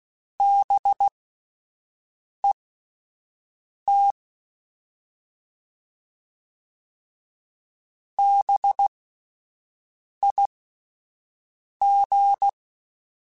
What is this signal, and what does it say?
Morse code, heard as a keyed tone, about 16 wpm, spaced as for 5 wpm, 800 Hz. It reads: BET BIG
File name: Alphabetical / Alphabetical mp3s / Bs @5